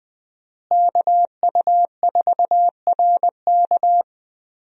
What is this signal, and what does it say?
Morse code, heard as a keyed tone, about 21 wpm, 700 Hz. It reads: KU4RK
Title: USA Call Signs - 5 Characters - Repeated 2x - 20wpm